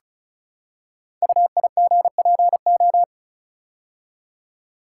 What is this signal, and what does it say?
Morse code, heard as a keyed tone, about 34 wpm, 700 Hz. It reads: UIGPO
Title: Code Groups - 5 Letters - 35wpm